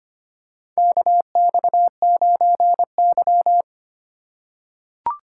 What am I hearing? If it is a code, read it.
Morse code, heard as a keyed tone, about 25 wpm, 700 Hz. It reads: KX9Y
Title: USA Call Signs - 4 Characters - 2x Character Spacing - 25wpm